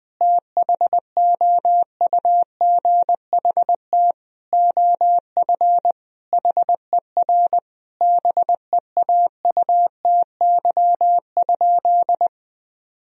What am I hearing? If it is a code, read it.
Morse code, heard as a keyed tone, about 20 wpm, 700 Hz. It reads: THOUGHT OF HER BEAUTY?
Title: Sentences from Top 1000 Words - Review Part 2 - 20wpm